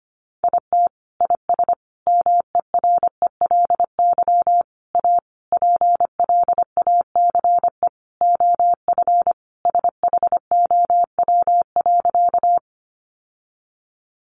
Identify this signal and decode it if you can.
Morse code, heard as a keyed tone, about 25 wpm, 700 Hz. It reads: IT IS MERELY A PLACE OF SHOW.
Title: Sentences from Top 1400 Words 25wpm